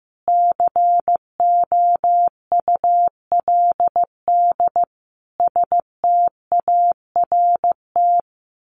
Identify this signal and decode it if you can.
Morse code, heard as a keyed tone, about 15 wpm, 700 Hz. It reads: COULD START
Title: A Sound Of Thunder 15wpm